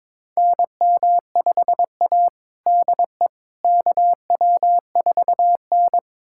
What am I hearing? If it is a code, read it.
Morse code, heard as a keyed tone, about 22 wpm, 700 Hz. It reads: NM5A DE KW4N
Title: USA Callsign DE Callsign - 4 Characters - 22wpm